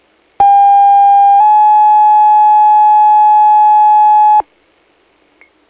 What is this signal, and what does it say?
Two-tone page, heard as a one-second tone, then a three-second tone, about 800 then 850 Hz. It is heard from 0.4 to 4.4 s.